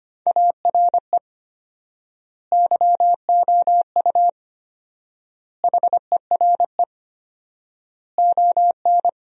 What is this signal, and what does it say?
Morse code, heard as a keyed tone, about 25 wpm, 700 Hz. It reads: ARE YOU HERE ON
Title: 4x Word-Spacing - Sentences from Top 400 Words - 25wpm